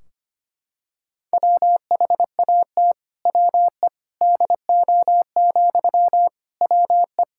Morse code, heard as a keyed tone, about 25 wpm, 700 Hz. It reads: WHAT WE DO, WE
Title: Sentences from Top 1600 Words 25wpm